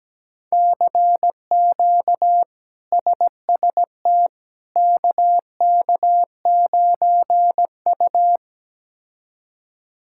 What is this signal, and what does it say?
Morse code, heard as a keyed tone, about 17 wpm, 700 Hz. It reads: CQ SST KK9U
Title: K1USN SST QSO Practice - 17wpm